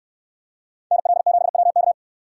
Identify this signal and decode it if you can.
Morse code, heard as a keyed tone, about 69 wpm, 700 Hz. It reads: NL7WK